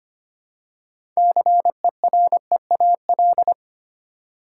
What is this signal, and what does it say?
Morse code, heard as a keyed tone, about 25 wpm, 690 Hz. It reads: CEREAL
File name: practice / 25wpm / Easy Words with Hints - 25wpm